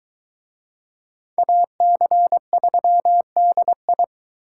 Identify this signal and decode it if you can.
Morse code, heard as a keyed tone, about 23 wpm, 700 Hz. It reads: AC3DI